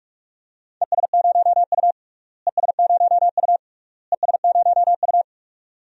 Morse code, heard as a keyed tone, about 41 wpm, 700 Hz. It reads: ES0U ES0U ES0U
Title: DXCC-100 Call Signs - 4 Characters - Repeated 3x - 45wpm